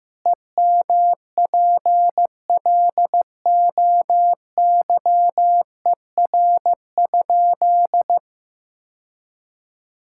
Morse code, heard as a keyed tone, about 15 wpm, 700 Hz. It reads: EMPLOYER?
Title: Sentences from Top 1600 Words 15wpm